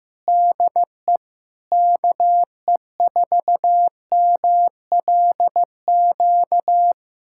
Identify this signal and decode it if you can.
Morse code, heard as a keyed tone, about 15 wpm, 700 Hz. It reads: DE KE4MLQ
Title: USA Callsign DE Callsign - 6 Characters - 15wpm